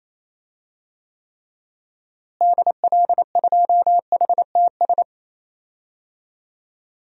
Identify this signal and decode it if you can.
Morse code, heard as a keyed tone, about 28 wpm, 700 Hz. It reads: DL2HTS
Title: DXCC-100 Call Signs - 6 Characters - Repeated 3x - 28wpm